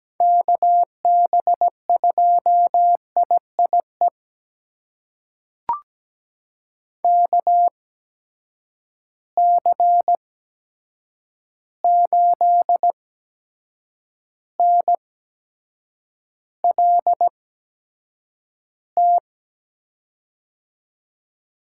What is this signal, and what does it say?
Morse code, heard as a keyed tone, about 17 wpm, spaced as for 7 wpm, 700 Hz. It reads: KB2IIE KC8NLT
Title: USA Call Signs - 6 Characters - 8x Character Spacing - 17wpm